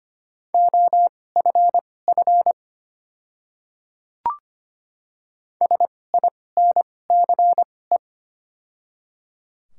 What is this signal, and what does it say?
Morse code, heard as a keyed tone, about 25 wpm, spaced as for 18 wpm, 700 Hz. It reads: OFF SINCE
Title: 2x Character Spacing - Top 500 Words - 25wpm